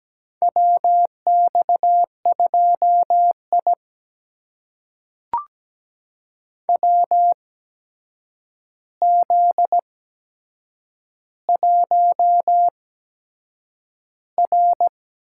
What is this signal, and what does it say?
Morse code, heard as a keyed tone, about 17 wpm, spaced as for 7 wpm, 700 Hz. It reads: WX2I WZ1R
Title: USA Call Signs - 4 Characters - 8x Character Spacing - 17wpm